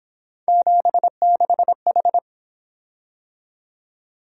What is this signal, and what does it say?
Morse code, heard as a keyed tone, about 28 wpm, 700 Hz. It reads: 76H
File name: Speed-Racing - Three Letter-Number Combinations - 22wpm